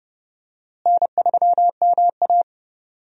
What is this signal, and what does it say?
Morse code, heard as a keyed tone, about 30 wpm, 700 Hz. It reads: N3MA
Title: USA Call Signs - 4 Characters - Repeated 3x - 30wpm